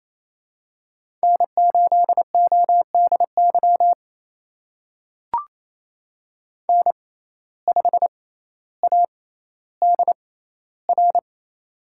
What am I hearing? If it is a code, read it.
Morse code, heard as a keyed tone, about 28 wpm, spaced as for 14 wpm, 700 Hz. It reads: N8ODY N5ADR
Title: USA Call Signs - 5 Characters - 6x Character Spacing - 28wpm